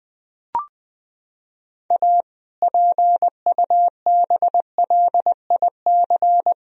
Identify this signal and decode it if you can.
Morse code, heard as a keyed tone, about 21 wpm, 700 Hz. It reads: A PUBLIC
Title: Sentences from Top 1000 Words - Review Part 3 - 20wpm